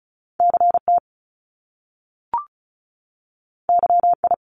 Morse code, heard as a keyed tone, about 36 wpm, 700 Hz. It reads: CT YI
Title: Top Two Letter Combinations in Words 35wpm